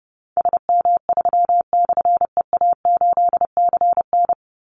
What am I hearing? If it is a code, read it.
Morse code, heard as a keyed tone, about 28 wpm, 700 Hz. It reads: SM3/EA8CN
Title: Call Signs Repeated 3x - 30wpm